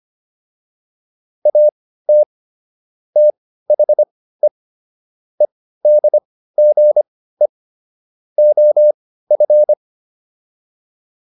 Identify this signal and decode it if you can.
Morse code, heard as a keyed tone, about 25 wpm, spaced as for 15 wpm, 600 Hz. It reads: AT THE EDGE OF